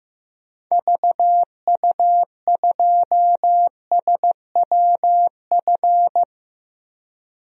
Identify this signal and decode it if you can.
Morse code, heard as a keyed tone, about 16 wpm, 700 Hz. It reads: VU2SWF
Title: DXCC-100 Call Signs - 6 Characters - Repeated 2x - 15wpm